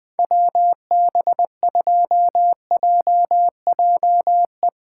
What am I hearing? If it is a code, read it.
Morse code, heard as a keyed tone, about 20 wpm, 700 Hz. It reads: WB2JJE
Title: Speed Racing - USA 2x3 Call Signs 17wpm